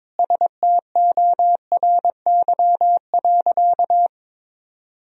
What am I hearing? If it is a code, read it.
Morse code, heard as a keyed tone, about 21 wpm, 700 Hz. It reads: STORY.
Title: Sentences from Top 300 Words and Abbreviations 22wpm